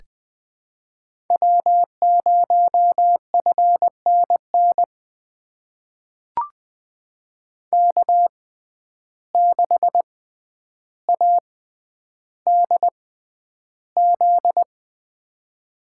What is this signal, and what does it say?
Morse code, heard as a keyed tone, about 20 wpm, spaced as for 10 wpm, 700 Hz. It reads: W0FNN K6ADZ
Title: USA Call Signs - 5 Characters - 6x Character Spacing - 20wpm